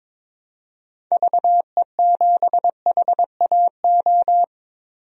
Morse code, heard as a keyed tone, about 21 wpm, 700 Hz. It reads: VE7HAO
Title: DXCC-100 Call Signs - 6 Characters - 22wpm